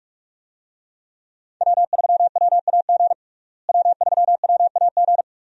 Morse code, heard as a keyed tone, about 47 wpm, 700 Hz. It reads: W3WAG W3WAG
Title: USA Call Signs - 5 Characters - Repeated 2x - 45wpm